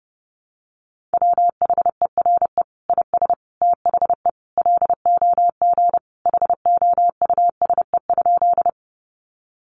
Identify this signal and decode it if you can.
Morse code, heard as a keyed tone, about 30 wpm, 700 Hz. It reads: WHERE IS THE LOG HOUSE?
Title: Sentences from Top 1400 Words 30wpm